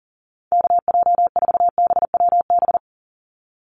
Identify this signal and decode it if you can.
Morse code, heard as a keyed tone, about 37 wpm, 700 Hz. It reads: KJ4BWB
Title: USA Call Signs 40wpm